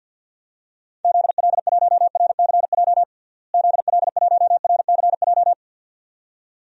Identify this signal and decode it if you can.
Morse code, heard as a keyed tone, about 50 wpm, 700 Hz. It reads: ZL1RCJ ZL1RCJ
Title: DXCC-100 Call Signs - 6 Characters - Repeated 2x - 50wpm